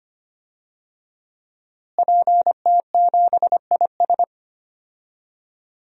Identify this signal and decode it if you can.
Morse code, heard as a keyed tone, about 25 wpm, 700 Hz. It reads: PT7IS
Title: DXCC-100 Call Signs - 5 Characters - Repeated 2x - 25wpm